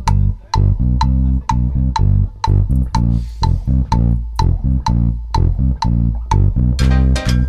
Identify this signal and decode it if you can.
There is no code.